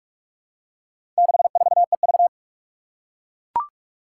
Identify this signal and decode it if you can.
Morse code, heard as a keyed tone, about 41 wpm, 700 Hz. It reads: 64EV
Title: Four Challenging Letter-Number Combinations 45wpm